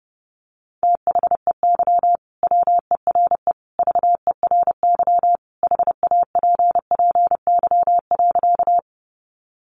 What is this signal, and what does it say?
Morse code, heard as a keyed tone, about 30 wpm, 700 Hz. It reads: THEY WERE VERY HAPPY.
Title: Sentences from Top 1000 Words 30wpm